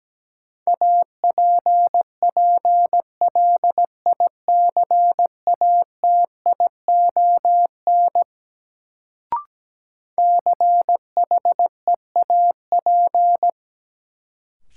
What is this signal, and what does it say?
Morse code, heard as a keyed tone, about 17 wpm, 700 Hz. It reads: APPLICATION CHEAP
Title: Sentences from Top 1200 Words - New Word Drill - 17wpm